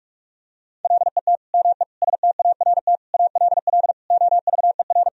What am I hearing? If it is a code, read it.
Morse code, heard as a keyed tone, about 41 wpm, 700 Hz. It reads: LET ME START ALL OVER
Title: Sentences from Top 1000 Words - Review Part 1 - 45wpm